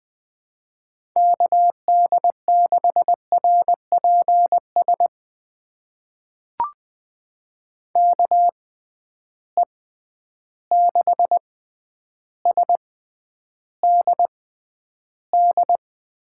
Morse code, heard as a keyed tone, about 20 wpm, spaced as for 10 wpm, 700 Hz. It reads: KD6RPS KE6SDD
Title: USA Call Signs - 6 Characters - 6x Character Spacing - 20wpm